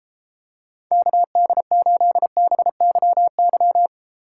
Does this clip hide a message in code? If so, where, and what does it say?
Yes: Morse code from 0.9 to 3.9 s, heard as a keyed tone, about 34 wpm, 700 Hz. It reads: KD8BYY